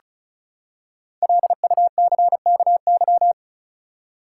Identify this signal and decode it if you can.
Morse code, heard as a keyed tone, about 34 wpm, 700 Hz. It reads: LUCKY
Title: Five Letter Words - 35wpm